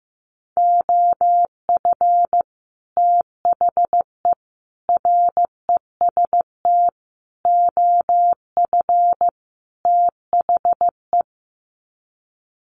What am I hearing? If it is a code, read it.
Morse code, heard as a keyed tone, about 15 wpm, 700 Hz. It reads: OF THE REST OF THE
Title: Sets of 5 Words 15wpm